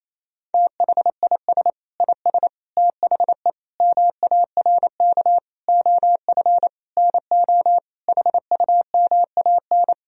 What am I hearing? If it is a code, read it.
Morse code, heard as a keyed tone, about 27 wpm, 700 Hz. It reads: THIS IS THE MARK OF NO HUMAN
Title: Sentences from Top 1000 Words - Review Part 3 - 28wpm